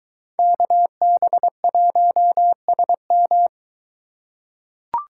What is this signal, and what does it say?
Morse code, heard as a keyed tone, about 22 wpm, 700 Hz. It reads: KB1SM